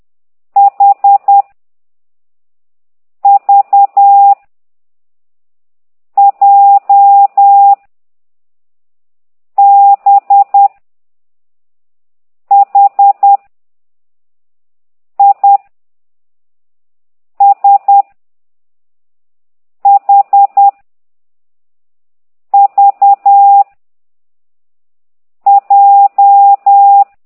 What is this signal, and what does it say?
Morse code, heard as a keyed tone, about 10 wpm, 800 Hz. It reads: H V J B H I S H V J
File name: ct2idw / exerc9